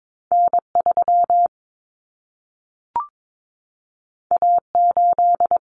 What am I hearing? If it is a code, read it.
Morse code, heard as a keyed tone, about 22 wpm, 700 Hz. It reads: N3 A8
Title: Common Cut Numbers - 2-Digit 22wpm